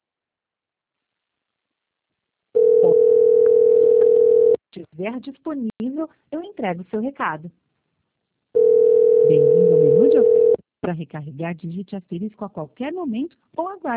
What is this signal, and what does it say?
Ringback tone (North America), 440 + 480 Hz, with speech.
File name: out- -2006- - - .40